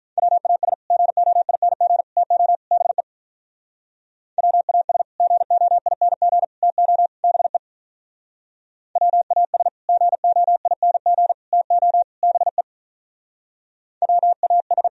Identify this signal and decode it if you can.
Morse code, heard as a keyed tone, about 44 wpm, 700 Hz. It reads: WAS GOING TO BE WAS GOING TO BE WAS GOING TO BE WAS